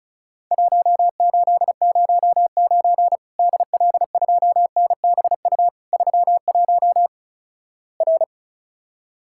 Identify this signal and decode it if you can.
Morse code, heard as a keyed tone, about 36 wpm, 700 Hz. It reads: 1809 DL2NBU 31 R
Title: Worked All Europe DX Contest - QTC Exchange - 35wpm